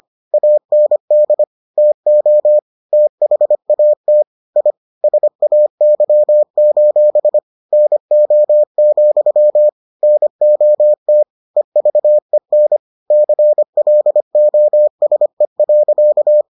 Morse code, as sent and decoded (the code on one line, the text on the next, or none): .- -. -.. / - --- / - .... .- - / .. / ... .- -.-- ---... / -. --- --..-- / -. --- - / . ...- . -. / -.-. .-.. --- ... .-.-.-
AND TO THAT I SAY: NO, NOT EVEN CLOSE.